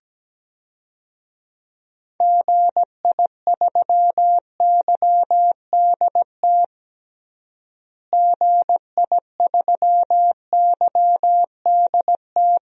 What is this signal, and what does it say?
Morse code, heard as a keyed tone, about 17 wpm, 700 Hz. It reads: GI3YDT GI3YDT
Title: DXCC-100 Call Signs - 6 Characters - Repeated 2x - 17wpm